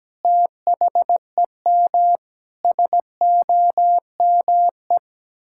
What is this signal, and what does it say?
Morse code, heard as a keyed tone, about 17 wpm, 700 Hz. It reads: THEM SOME
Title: Sentences from Top 1000 Words - Review Part 6 - 17wpm